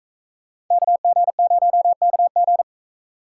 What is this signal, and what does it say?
Morse code, heard as a keyed tone, about 41 wpm, 700 Hz. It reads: KG0KG